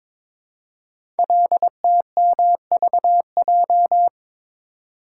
Morse code, heard as a keyed tone, about 22 wpm, 700 Hz. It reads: LTMVJ
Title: Code Groups - 5 Letters - 22wpm